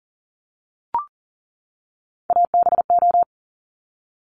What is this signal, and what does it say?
Morse code, heard as a keyed tone, about 40 wpm, 700 Hz. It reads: ABO
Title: Top Three Letter Combinations in Words 40wpm